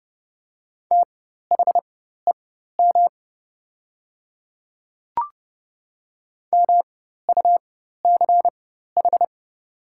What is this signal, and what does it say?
Morse code, heard as a keyed tone, about 30 wpm, spaced as for 14 wpm, 700 Hz. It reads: THEM MUCH
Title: 4x Character Spacing - Top 200 Words - 30wpm